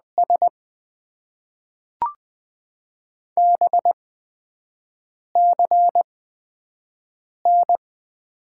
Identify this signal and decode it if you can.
Morse code, heard as a keyed tone, about 20 wpm, spaced as for 5 wpm, 700 Hz. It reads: S BCN
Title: Top 90 QSO Elements - 8x Character Spacing - 20wpm